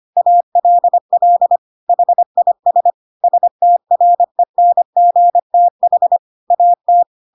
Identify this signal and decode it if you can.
Morse code, heard as a keyed tone, about 25 wpm, 700 Hz. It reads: ALL HIS STRENGTH AT
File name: Pinocchio / Pinocchio ch4 25 25 ik1hge